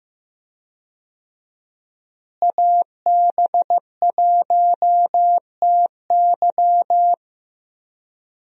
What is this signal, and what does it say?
Morse code, heard as a keyed tone, about 15 wpm, 700 Hz. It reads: AB1TY
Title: USA Call Signs - 5 Characters - Repeated 2x - 15wpm